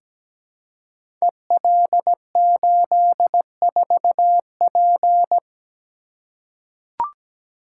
Morse code, heard as a keyed tone, about 17 wpm, 700 Hz. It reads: EL84P
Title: Code Groups - 5 Letter:Numbers - 17wpm